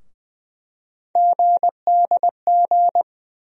Morse code, heard as a keyed tone, about 20 wpm, 700 Hz. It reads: GDG
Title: USA Call Sign Postfixes - 3 Letters 20wpm